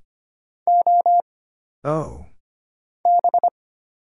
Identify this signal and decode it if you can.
Morse code, heard as a keyed tone, about 25 wpm, 700 Hz. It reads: O B
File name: Single Character (Letter:Number:Punctuation:Prosign) - Mind-Melt - 25wpm